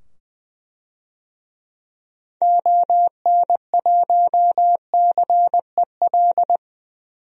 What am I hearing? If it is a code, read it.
Morse code, heard as a keyed tone, about 20 wpm, 700 Hz. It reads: ON1CEL
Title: DXCC-100 Call Signs - 6 Characters - 20wpm